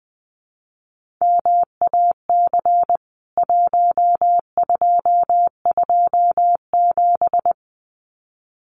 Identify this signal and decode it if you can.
Morse code, heard as a keyed tone, about 19 wpm, 700 Hz. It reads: MAC 1227
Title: CWT Exchange 20wpm